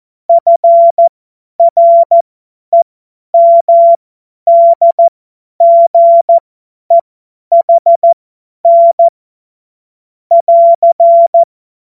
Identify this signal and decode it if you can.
Morse code, heard as a keyed tone, about 14 wpm, spaced as for 10 wpm, 680 Hz. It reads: FREMDGEHN <AR>
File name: Morsekurs / Witze / Witz 12 Dezember 2023 50 F70